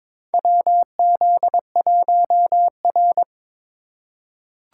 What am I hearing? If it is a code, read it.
Morse code, heard as a keyed tone, about 22 wpm, 700 Hz. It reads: WZ1R
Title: USA Call Signs - 4 Characters - 4x Character Spacing - 22wpm